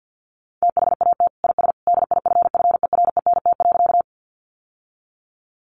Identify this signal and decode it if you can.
Morse code, heard as a keyed tone, about 50 wpm, 700 Hz. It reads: THAT IS DIFFERENT.
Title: Sentences from Top 600 Words 50wpm